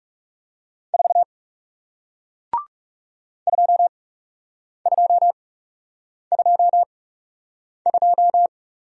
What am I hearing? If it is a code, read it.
Morse code, heard as a keyed tone, about 40 wpm, spaced as for 8 wpm, 700 Hz. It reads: 4 2222